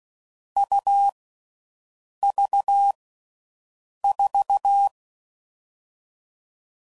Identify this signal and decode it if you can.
Morse code, heard as a keyed tone, about 16 wpm, 800 Hz. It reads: U V 4
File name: Difficult Combos